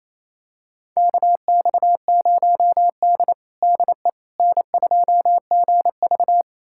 Morse code, heard as a keyed tone, about 27 wpm, 690 Hz. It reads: KX0D DE N2GV
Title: USA Callsign DE Callsign - 4 Characters - 28wpm